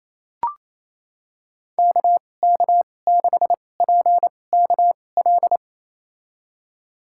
Morse code, heard as a keyed tone, about 28 wpm, spaced as for 20 wpm, 700 Hz. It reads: KK6PKL
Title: USA Call Signs - 6 Characters - 2x Character Spacing - 28wpm